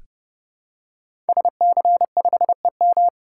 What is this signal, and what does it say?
Morse code, heard as a keyed tone, about 30 wpm, 700 Hz. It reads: SC5EM
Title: Code Groups - 5 Letter:Numbers - 30wpm